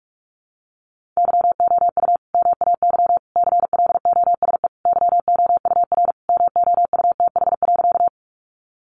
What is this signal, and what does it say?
Morse code, heard as a keyed tone, about 44 wpm, 700 Hz. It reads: YOU MAY CLOSE YOUR MOUTH.
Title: Sentences from Top 1500 Words 45wpm